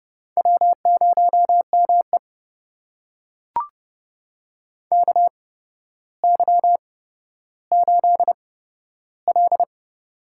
Morse code, heard as a keyed tone, about 28 wpm, spaced as for 13 wpm, 700 Hz. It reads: W0ME KY8L
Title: USA Call Signs - 4 Characters - 8x Character Spacing - 30wpm